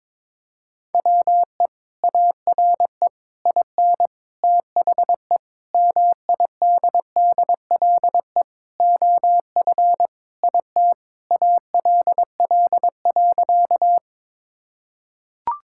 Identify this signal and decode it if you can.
Morse code, heard as a keyed tone, about 22 wpm, 700 Hz. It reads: WE ARE IN THE MIDDLE OF IT ALL.
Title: Sentences from Top 1000 Words - Review Part 6 - 22wpm